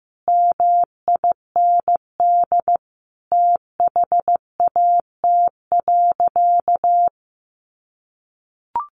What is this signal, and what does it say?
Morse code, heard as a keyed tone, about 15 wpm, 700 Hz. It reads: MIND THAT.
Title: Sentences from Top 500 Words 15wpm